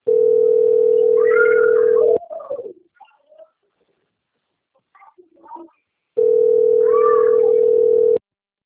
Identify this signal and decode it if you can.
Ringback tone (North America), 440 + 480 Hz.